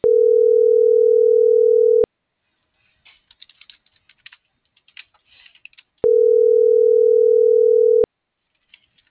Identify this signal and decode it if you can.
Ringback tone (North America), 440 + 480 Hz.